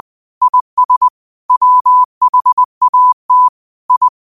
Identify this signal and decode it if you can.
Morse code, heard as a keyed tone, about 20 wpm, 1000 Hz. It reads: IS WHAT I
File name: qotdfiles / 20wpm